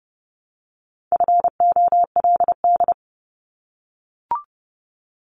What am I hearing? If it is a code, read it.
Morse code, heard as a keyed tone, about 30 wpm, 700 Hz. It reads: FOLD
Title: Multi-letter Phonemes Repeated 3x 30wpm